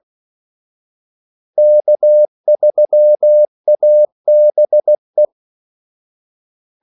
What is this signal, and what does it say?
Morse code, heard as a keyed tone, about 16 wpm, 600 Hz. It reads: K3ABE
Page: Kaldesignaler 16 wpm